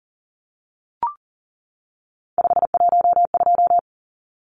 Morse code, heard as a keyed tone, about 40 wpm, 700 Hz. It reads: 512
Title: Numbers - 3-Digit 40wpm